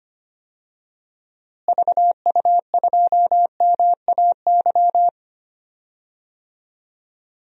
Morse code, heard as a keyed tone, about 25 wpm, 700 Hz. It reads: VU2MAY
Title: DXCC-100 Call Signs - 6 Characters - Repeated 2x - 25wpm